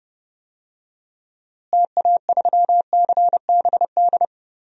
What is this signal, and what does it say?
Morse code, heard as a keyed tone, about 32 wpm, 700 Hz. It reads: TA3CBD
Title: DXCC-100 Call Signs - 6 Characters - Repeated 3x - 30wpm